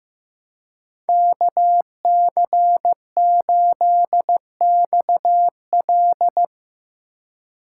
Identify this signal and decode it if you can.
Morse code, heard as a keyed tone, about 15 wpm, 700 Hz. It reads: KC8XL
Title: Code Groups - 5 Letter:Numbers - 15wpm